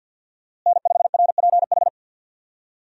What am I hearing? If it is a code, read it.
Morse code, heard as a keyed tone, about 51 wpm, 700 Hz. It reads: N5RPH
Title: USA Call Signs - 5 Characters - Repeated 2x - 50wpm